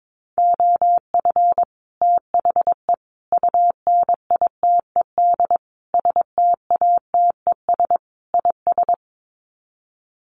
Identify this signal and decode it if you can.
Morse code, heard as a keyed tone, about 22 wpm, 700 Hz. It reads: OF THE UNITED STATES IS
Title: Sets of 5 Words Encore2 22wpm